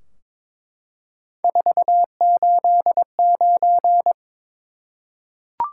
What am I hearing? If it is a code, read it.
Morse code, heard as a keyed tone, about 21 wpm, 700 Hz. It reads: 489
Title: Top 180 QSO Elements 22wpm